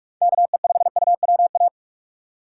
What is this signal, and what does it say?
Morse code, heard as a keyed tone, about 45 wpm, 700 Hz. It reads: KE5UWA
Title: USA Call Signs - 6 Characters - 2x Character Spacing - 45wpm